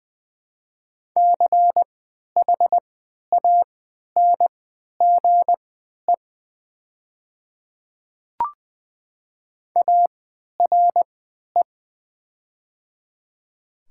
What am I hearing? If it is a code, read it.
Morse code, heard as a keyed tone, about 20 wpm, spaced as for 11 wpm, 700 Hz. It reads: CHANGE ARE
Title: 3x Character Spacing - Top 500 Words - 20wpm